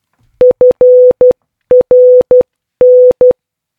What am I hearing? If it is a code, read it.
Morse code, heard as a keyed tone, about 12 wpm, 500 Hz. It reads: FRN